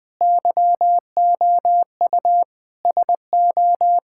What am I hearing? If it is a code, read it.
Morse code, heard as a keyed tone, about 21 wpm, 700 Hz. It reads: YOU SO
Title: Sentences with 4-Word Phrases - Spoken Beginning - 20wpm